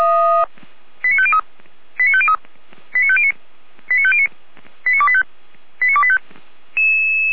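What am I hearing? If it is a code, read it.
Five-tone selective call (ZVEI), heard as quick runs of five tones, tones of about 70 ms. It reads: 89503 89503 89509 89509 89285 89285
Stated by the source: ZVEI five-tone selective call received on an SDR.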